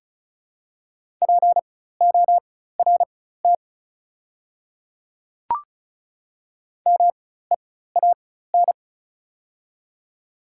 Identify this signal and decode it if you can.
Morse code, heard as a keyed tone, about 36 wpm, spaced as for 16 wpm, 700 Hz. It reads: PORT MEAN
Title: 4x Character Spacing - Top 200 Words - 35wpm